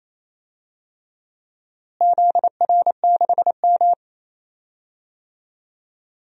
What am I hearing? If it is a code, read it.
Morse code, heard as a keyed tone, about 28 wpm, 700 Hz. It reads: ZR6M
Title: DXCC-100 Call Signs - 4 Characters - Repeated 2x - 28wpm